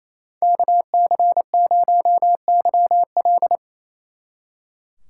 Morse code, heard as a keyed tone, about 33 wpm, 700 Hz. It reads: KC0YL
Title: Speed Racing - USA 2x2 Call Signs 28wpm